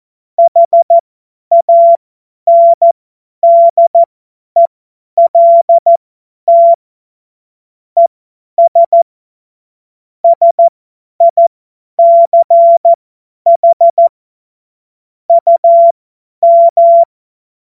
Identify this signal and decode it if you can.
Morse code, heard as a keyed tone, about 14 wpm, spaced as for 10 wpm, 680 Hz. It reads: HANDELT ES SICH UM
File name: Morsekurs / Witze / Witz 14. Juni 2022 50 F70